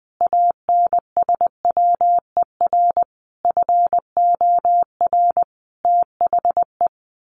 Morse code, heard as a keyed tone, about 20 wpm, 700 Hz. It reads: ANSWER FOR THE
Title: Sentences from Top 1400 Words 20wpm